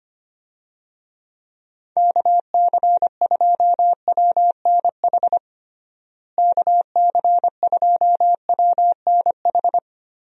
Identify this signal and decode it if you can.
Morse code, heard as a keyed tone, about 25 wpm, 700 Hz. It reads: KC2WNH KC2WNH
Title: USA Call Signs - 6 Characters - Repeated 3x - 25wpm